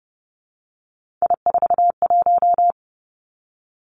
Morse code, heard as a keyed tone, about 30 wpm, 700 Hz. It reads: I41
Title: Three Challenging Letter-Number Combinations 30wpm